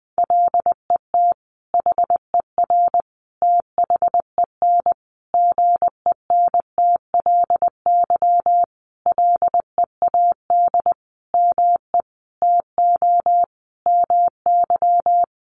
Morse code, heard as a keyed tone, about 20 wpm, 700 Hz. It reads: LET HER THEN GENTLY LEAD ME TO MY
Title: Sentences from Top 1500 Words 20wpm